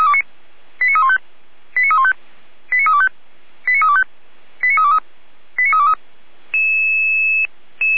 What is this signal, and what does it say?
Five-tone selective call (ZVEI), heard as quick runs of five tones, tones of about 70 ms. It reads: 89315 89315 89325 89325 89323 89323